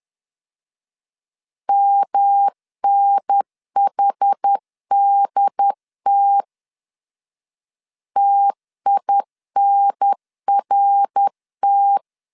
Morse code, heard as a keyed tone, about 10 wpm, 800 Hz. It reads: MNHDT TINRT
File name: CW / DET / 1 3 1500C S C 10wpm 3335